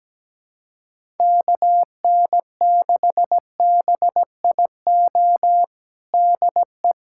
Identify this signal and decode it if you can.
Morse code, heard as a keyed tone, about 17 wpm, 700 Hz. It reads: KN6BIO DE
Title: USA Callsign DE Callsign - 6 Characters - 17wpm